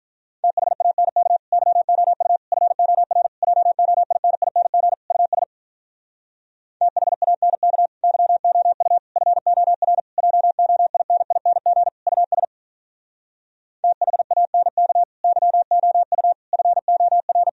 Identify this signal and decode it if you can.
Morse code, heard as a keyed tone, about 46 wpm, 700 Hz. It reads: THANK YOU FOR JOINING US THANK YOU FOR JOINING US THANK YOU FOR